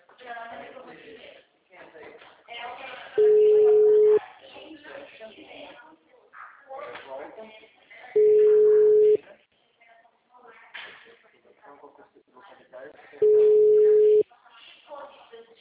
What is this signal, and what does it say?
Ringback tone (Europe, 425 Hz), with speech.